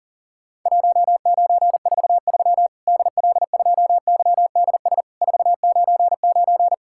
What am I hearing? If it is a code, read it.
Morse code, heard as a keyed tone, about 40 wpm, 700 Hz. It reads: 1943 DL2YDS 499
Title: Worked All Europe DX Contest - QTC Exchange - 40wpm